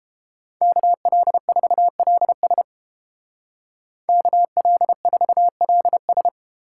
Morse code, heard as a keyed tone, about 30 wpm, 700 Hz. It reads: KL4LS KL4LS